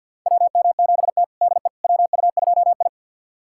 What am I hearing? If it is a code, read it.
Morse code, heard as a keyed tone, about 52 wpm, 690 Hz. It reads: WM7T DE WU2I
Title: USA Callsign DE Callsign - 4 Characters - 50wpm